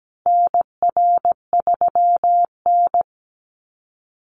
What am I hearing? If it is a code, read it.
Morse code, heard as a keyed tone, about 17 wpm, 700 Hz. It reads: NR3N
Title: Call Signs Repeated 3x - 17wpm